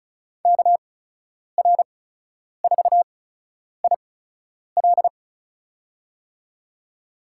Morse code, heard as a keyed tone, about 36 wpm, spaced as for 10 wpm, 700 Hz. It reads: KR4IL
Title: USA Call Signs - 8x Character Spacing - 35wpm